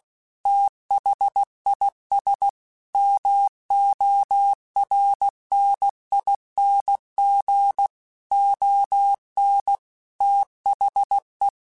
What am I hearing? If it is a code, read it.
Morse code, heard as a keyed tone, about 16 wpm, 800 Hz. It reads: THIS MORNING ON THE
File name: Weather Sets / Weather Sets-mp3s / Weather 4 24 @15